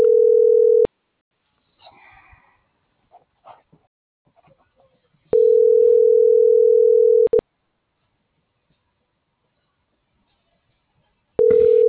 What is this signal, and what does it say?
Ringback tone (North America), 440 + 480 Hz.